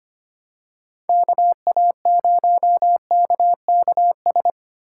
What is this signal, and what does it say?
Morse code, heard as a keyed tone, about 25 wpm, 700 Hz. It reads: KA0KKS
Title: USA Call Signs - 6 Characters - 8x Character Spacing - 25wpm